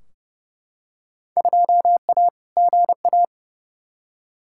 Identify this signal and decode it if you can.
Morse code, heard as a keyed tone, about 30 wpm, 700 Hz. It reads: 2A GA
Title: Field Day Exchanges - 30wpm